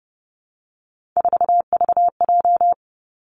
Morse code, heard as a keyed tone, about 30 wpm, 700 Hz. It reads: 4VJ
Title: Three Challenging Letter-Number Combinations 30wpm